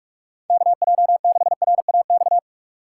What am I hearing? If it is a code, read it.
Morse code, heard as a keyed tone, about 46 wpm, 700 Hz. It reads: KJ6RAX
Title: USA Call Signs - 6 Characters - 3x Character Spacing - 45wpm